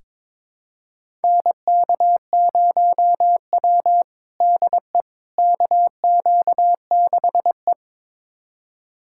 Morse code, heard as a keyed tone, about 22 wpm, 700 Hz. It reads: NK0W DE KQ6E
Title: USA Callsign DE Callsign - 4 Characters - 22wpm